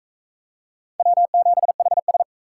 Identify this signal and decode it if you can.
Morse code, heard as a keyed tone, about 36 wpm, 700 Hz. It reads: W7HS